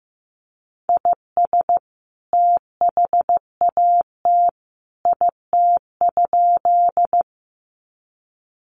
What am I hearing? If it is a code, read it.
Morse code, heard as a keyed tone, about 14 wpm, 700 Hz. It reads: IS THAT IT?
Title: Sentences from Top 100 Words 15wpm